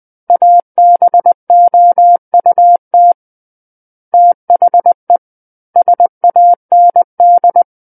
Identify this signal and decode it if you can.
Morse code, heard as a keyed tone, about 20 wpm, 700 Hz. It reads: ABOUT THE SAND
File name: 20 WPM